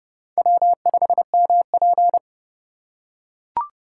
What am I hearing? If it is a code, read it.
Morse code, heard as a keyed tone, about 28 wpm, 700 Hz. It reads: W5MP
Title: USA Call Signs - 4 Characters - 6x Character Spacing - 30wpm